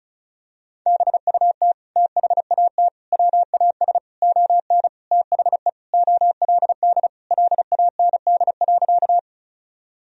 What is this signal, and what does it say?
Morse code, heard as a keyed tone, about 36 wpm, 700 Hz. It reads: BUT THAT WAS ON THE OLD LAND.
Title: Sentences from Top 1000 Words - Review Part 1 - 35wpm